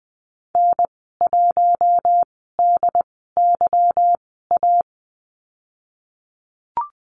Morse code, heard as a keyed tone, about 20 wpm, spaced as for 15 wpm, 700 Hz. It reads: N1DYA
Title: 147 Lesson 24g - Letter y - Call Signs 20wpm